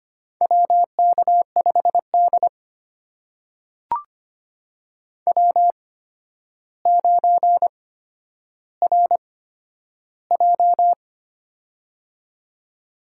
Morse code, heard as a keyed tone, about 25 wpm, spaced as for 11 wpm, 700 Hz. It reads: WK5D W9RJ
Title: USA Call Signs - 4 Characters - 8x Character Spacing - 25wpm